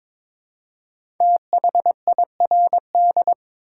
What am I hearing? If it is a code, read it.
Morse code, heard as a keyed tone, about 22 wpm, 700 Hz. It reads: THIRD
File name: 5 Letter Words - First 3 Letters Given - 22wpm